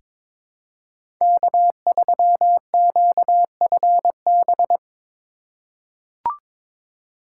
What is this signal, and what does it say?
Morse code, heard as a keyed tone, about 22 wpm, 700 Hz. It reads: K3QFB
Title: USA Call Signs - 2x Character Spacing - 22wpm